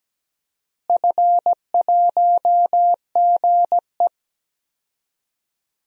Morse code, heard as a keyed tone, about 17 wpm, 700 Hz. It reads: F1GE
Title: DXCC-100 Call Signs - 4 Characters - Repeated 3x - 17wpm